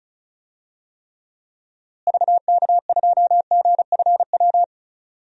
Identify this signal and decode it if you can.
Morse code, heard as a keyed tone, about 36 wpm, 700 Hz. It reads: VK2GFW
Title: DXCC-100 Call Signs - 6 Characters - Repeated 2x - 35wpm